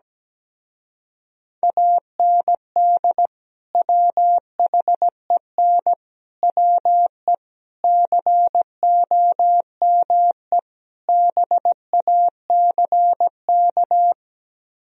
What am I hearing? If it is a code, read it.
Morse code, heard as a keyed tone, about 17 wpm, 700 Hz. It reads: AND WHEN WE COME BACK
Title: Speed-Racing - Sets of 5 Words 17wpm